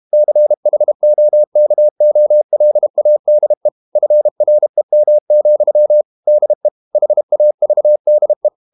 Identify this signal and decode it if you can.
Morse code, heard as a keyed tone, about 32 wpm, 600 Hz. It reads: CHOKOLADE FREM, DE HAVDE
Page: Ferie 32 wpm